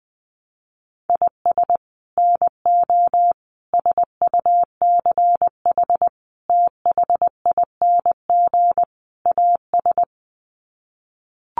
Morse code, heard as a keyed tone, about 20 wpm, 700 Hz. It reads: IS NO SUCH THING AS
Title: Sets of 5 Words Encore 20wpm